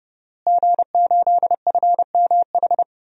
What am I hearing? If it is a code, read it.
Morse code, heard as a keyed tone, about 31 wpm, 700 Hz. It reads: G8FMH
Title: DXCC-100 Call Signs - 5 Characters - Repeated 3x - 30wpm